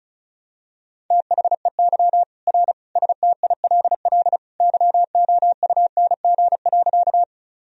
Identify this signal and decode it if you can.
Morse code, heard as a keyed tone, about 36 wpm, 700 Hz. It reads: THEY R STILL YOUNG.
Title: Sentences from Top 400 Words and Abbreviations 35wpm